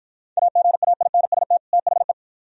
Morse code, heard as a keyed tone, about 54 wpm, 700 Hz. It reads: AGAINST THE